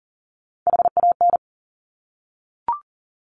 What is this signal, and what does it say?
Morse code, heard as a keyed tone, about 41 wpm, 690 Hz. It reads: HAN
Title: Top Three Letter Combinations in Words 40wpm